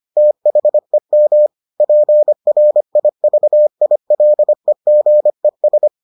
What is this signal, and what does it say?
Morse code, heard as a keyed tone, about 25 wpm, 600 Hz. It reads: THEM PRIVILEGES